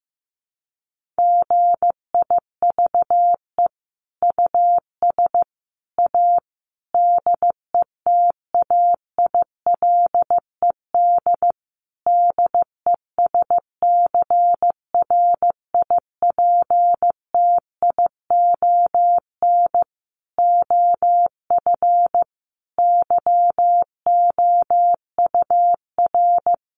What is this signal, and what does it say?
Morse code, heard as a keyed tone, about 14 wpm, 700 Hz. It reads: GIVE US A DETAILED DESCRIPTION OF YOUR
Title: Sentences from Top 1500 Words 15wpm